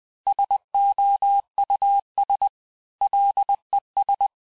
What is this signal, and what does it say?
Morse code, heard as a keyed tone, about 20 wpm, 800 Hz. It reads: SOUS LES
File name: cours-du-mercredi / audios / cours-858-texte-clais-vitesse-19mpm